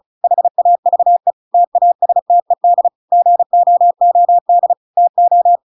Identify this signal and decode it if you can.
Morse code, heard as a keyed tone, about 36 wpm, 700 Hz. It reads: HAVE TASTED GOOD TO